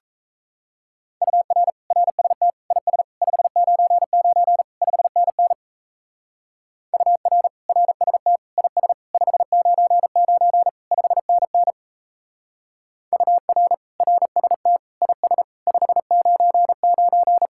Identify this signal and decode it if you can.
Morse code, heard as a keyed tone, about 38 wpm, 700 Hz. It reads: UR RST IS 599 5NN UR RST IS 599 5NN UR RST IS 599